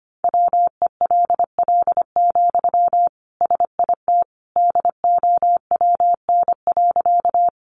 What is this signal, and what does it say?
Morse code, heard as a keyed tone, about 25 wpm, 700 Hz. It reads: WELL, SIT DOWN.
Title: Sentences from Top 500 Words 25wpm